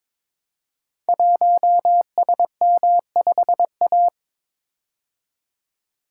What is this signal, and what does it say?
Morse code, heard as a keyed tone, about 22 wpm, 700 Hz. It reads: 1SM5A
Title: Code Groups - 5 Letter:Numbers - 22wpm